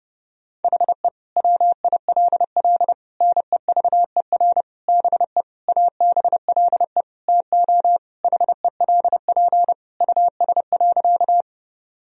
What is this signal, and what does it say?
Morse code, heard as a keyed tone, about 32 wpm, 700 Hz. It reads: HE WILL NEVER BE ABLE TO HELP US.
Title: Sentences from Top 1000 Words - Review Part 2 - 30wpm